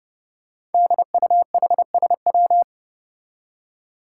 Morse code, heard as a keyed tone, about 32 wpm, 700 Hz. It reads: DUHSW
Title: Five Challenging Letter-Number Combinations 30wpm